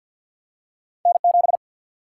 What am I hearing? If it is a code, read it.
Morse code, heard as a keyed tone, about 53 wpm, 700 Hz. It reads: N7
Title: Single-Letter Single-Number 50wpm